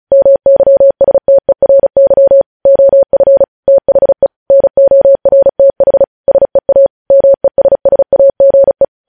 Morse code, heard as a keyed tone, about 36 wpm, 550 Hz. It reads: MYSTERY OF THE NORTH SEA MESSAGE
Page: news headlines in morse code at 35 wpm